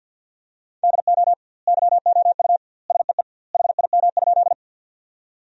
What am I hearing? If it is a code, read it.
Morse code, heard as a keyed tone, about 50 wpm, 700 Hz. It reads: DO YOU SEE HIM?
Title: 2x Word-Spacing - Sentences from Top 100 Words - 50wpm